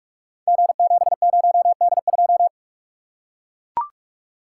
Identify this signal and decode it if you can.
Morse code, heard as a keyed tone, about 44 wpm, 700 Hz. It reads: G70D2
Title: Code Groups - 5 Letter:Numbers - 45wpm